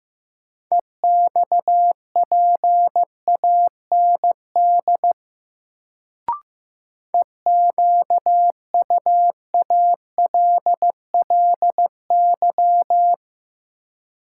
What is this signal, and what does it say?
Morse code, heard as a keyed tone, about 16 wpm, 700 Hz. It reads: EXPAND EQUALLY
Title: Sentences from Top 1400 Words - New Word Drill - 15wpm